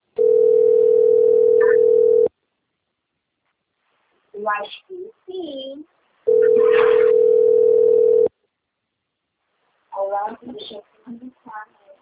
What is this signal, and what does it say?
Ringback tone (North America), 440 + 480 Hz, with speech.
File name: rg-1600-unknown- - - .17393